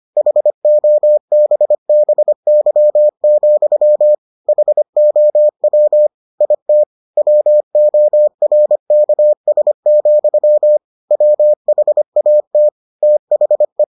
Morse code, heard as a keyed tone, about 25 wpm, 600 Hz. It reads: HOBBY, HOW IT WORKS, WHAT THE